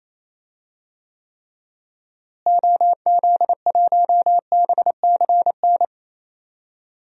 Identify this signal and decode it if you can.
Morse code, heard as a keyed tone, about 27 wpm, 700 Hz. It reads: OZ1BCN
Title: DXCC-100 Call Signs - 6 Characters - Repeated 3x - 28wpm